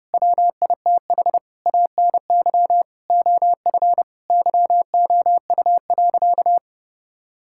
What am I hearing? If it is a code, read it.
Morse code, heard as a keyed tone, about 30 wpm, 700 Hz. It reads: WITH ANY OF YOU.
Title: Sentences from Top 1600 Words 30wpm